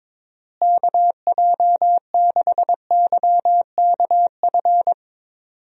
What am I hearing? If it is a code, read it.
Morse code, heard as a keyed tone, about 22 wpm, 700 Hz. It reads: KJ6YKF
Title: USA Call Signs - 6 Characters - Repeated 2x - 22wpm